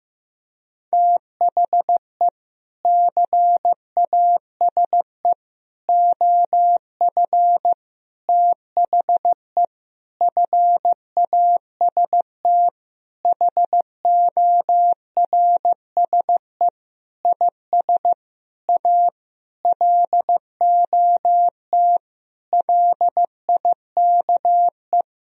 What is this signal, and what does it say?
Morse code, heard as a keyed tone, about 15 wpm, 700 Hz. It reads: THE CASE OF THE FAST HORSE IS A LOT LIKE
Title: Sentences from Top 1000 Words - Review Part 2 - 15wpm